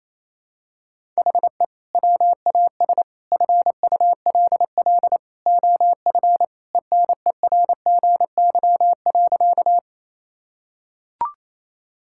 Morse code, heard as a keyed tone, about 28 wpm, 700 Hz. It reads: HE WAS FULL OF ENERGY.
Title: Sentences from Top 1000 Words - Review Part 6 - 28wpm